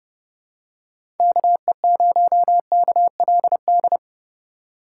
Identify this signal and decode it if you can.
Morse code, heard as a keyed tone, about 31 wpm, 700 Hz. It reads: KE0KLD
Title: USA Call Signs - 6 Characters - 3x Character Spacing - 30wpm